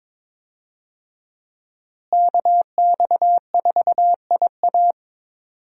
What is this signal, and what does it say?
Morse code, heard as a keyed tone, about 22 wpm, 700 Hz. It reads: KX4IA
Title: USA Call Signs - 5 Characters - Repeated 3x - 22wpm